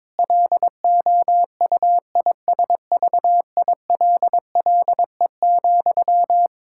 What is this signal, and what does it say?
Morse code, heard as a keyed tone, about 22 wpm, 700 Hz. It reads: LOUISVILLE,
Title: K1USN SST QSO Practice 22wpm